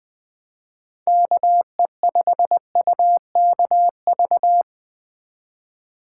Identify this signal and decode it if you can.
Morse code, heard as a keyed tone, about 20 wpm, 700 Hz. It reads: KE5UKV
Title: Speed Racing - USA 2x3 Call Signs 17wpm